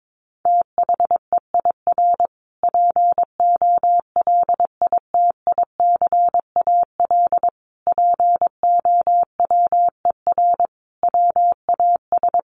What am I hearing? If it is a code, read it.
Morse code, heard as a keyed tone, about 22 wpm, 700 Hz. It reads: THEIR POLITICAL POWER WAS
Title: Sentences from Top 1300 Words 22wpm